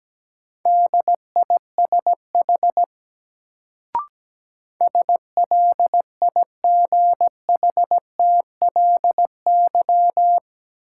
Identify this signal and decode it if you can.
Morse code, heard as a keyed tone, about 17 wpm, 700 Hz. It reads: DISH SLIGHTLY
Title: Sentences from Top 1300 Words - New Word Drill - 17wpm